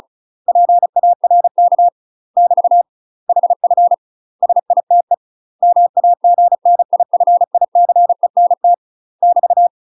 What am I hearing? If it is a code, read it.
Morse code, heard as a keyed tone, about 34 wpm, 700 Hz. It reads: PARK <BT> HF SITE MAGNIFICENT <BT>
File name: UK MASTER QSOs TRAINING/35WPM